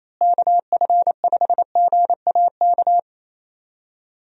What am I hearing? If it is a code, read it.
Morse code, heard as a keyed tone, about 28 wpm, 700 Hz. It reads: KF5GAK
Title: USA Call Signs - 6 Characters - Repeated 3x - 28wpm